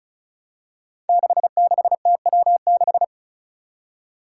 Morse code, heard as a keyed tone, about 36 wpm, 700 Hz. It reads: 66TW6
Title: Five Challenging Letter-Number Combinations 35wpm